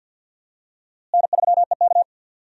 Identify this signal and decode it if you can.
Morse code, heard as a keyed tone, about 51 wpm, 700 Hz. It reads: N3EK
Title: USA Call Signs - 4 Characters - Repeated 2x - 50wpm